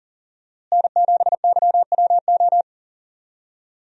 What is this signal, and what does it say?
Morse code, heard as a keyed tone, about 40 wpm, 700 Hz. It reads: N7YWO
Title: USA Call Signs - 5 Characters - Repeated 3x - 40wpm